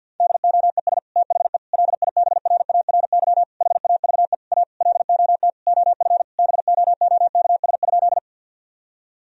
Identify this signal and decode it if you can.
Morse code, heard as a keyed tone, about 50 wpm, 700 Hz. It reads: DOES THE LIBRARY HAVE A LOT OF BOOKS?
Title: Sentences from Top 1000 Words - Review Part 4 - 50wpm